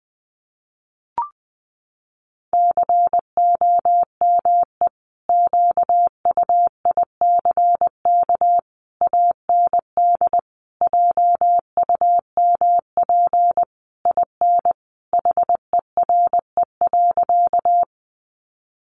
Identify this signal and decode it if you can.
Morse code, heard as a keyed tone, about 20 wpm, 700 Hz. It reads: COME QUICK AND JUMP IN HERE.
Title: Sentences from Top 1300 Words 20wpm